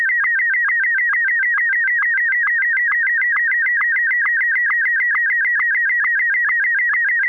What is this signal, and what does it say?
SSTV (slow-scan television), Martin 1.